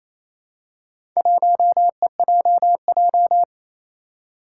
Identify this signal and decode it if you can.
Morse code, heard as a keyed tone, about 28 wpm, 700 Hz. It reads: 1EJJ
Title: Four Challenging Letter-Number Combinations 28wpm